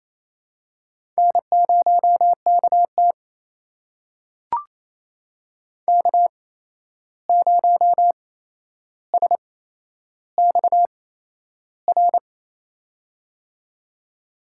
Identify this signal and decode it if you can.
Morse code, heard as a keyed tone, about 28 wpm, spaced as for 11 wpm, 700 Hz. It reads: N0KT K0SXR
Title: USA Call Signs - 8x Character Spacing - 28wpm